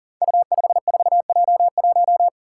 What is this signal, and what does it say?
Morse code, heard as a keyed tone, about 38 wpm, 700 Hz. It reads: U54J1
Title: Five Challenging Letter-Number Combinations 40wpm